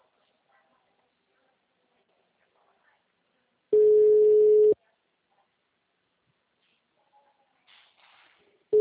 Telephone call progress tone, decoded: ringback tone

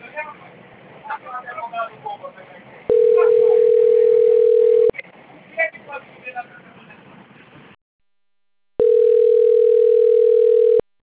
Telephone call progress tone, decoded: ringback tone

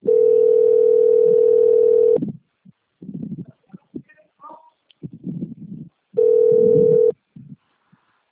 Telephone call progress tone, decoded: ringback tone

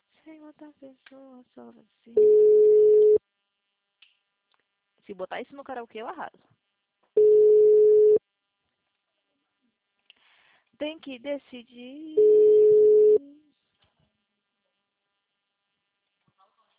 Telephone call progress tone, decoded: ringback tone

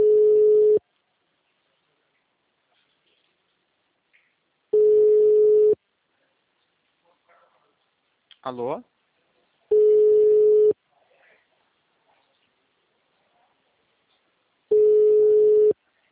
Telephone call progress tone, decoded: ringback tone

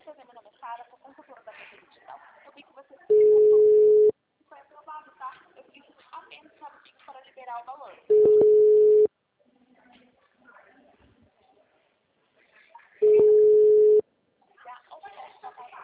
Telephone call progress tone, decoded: ringback tone

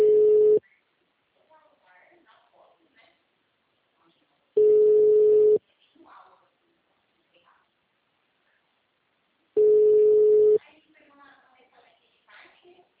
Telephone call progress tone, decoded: ringback tone

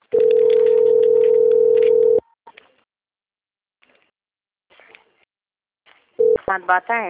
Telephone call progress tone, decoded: ringback tone

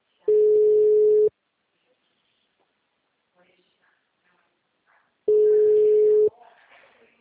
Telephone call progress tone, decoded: ringback tone

